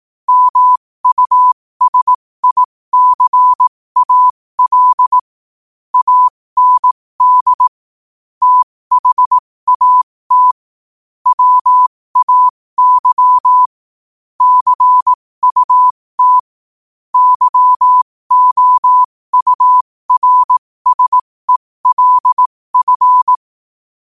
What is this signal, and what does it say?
Morse code, heard as a keyed tone, about 18 wpm, spaced as for 15 wpm, 1000 Hz. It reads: MUSICAL AND THAT WAY CUT YOURSELF
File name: qotdfiles / 15wpm